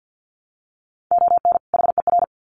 Morse code, heard as a keyed tone, about 50 wpm, 700 Hz. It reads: ON HER
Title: Sets of 2 Words 50wpm